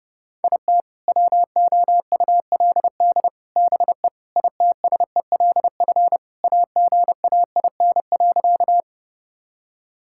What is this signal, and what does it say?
Morse code, heard as a keyed tone, about 30 wpm, 700 Hz. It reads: IT WOULD BE ITSELF AGAIN.
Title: Sentences from Top 1000 Words - Review Part 5 - 30wpm